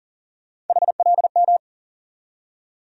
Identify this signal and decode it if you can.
Morse code, heard as a keyed tone, about 40 wpm, 700 Hz. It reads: HLM
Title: USA Call Sign Postfixes - 3 Letters 40wpm